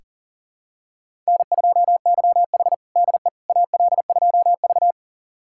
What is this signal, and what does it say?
Morse code, heard as a keyed tone, about 40 wpm, 700 Hz. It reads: N2YH DE AL2V
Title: USA Callsign DE Callsign - 4 Characters - 40wpm